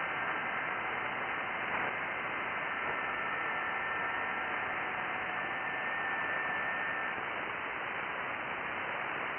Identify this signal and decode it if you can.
FT8.
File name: IC-7100